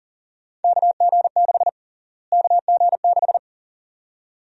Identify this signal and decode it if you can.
Morse code, heard as a keyed tone, about 40 wpm, 700 Hz. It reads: KG6 KG6
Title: USA Call Sign Prefixes - Repeated 2x - 40wpm